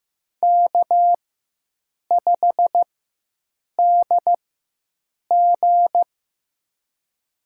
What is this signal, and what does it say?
Morse code, heard as a keyed tone, about 15 wpm, spaced as for 7 wpm, 700 Hz. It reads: K5DG
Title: USA Call Signs - 4 Characters - 4x Character Spacing - 15wpm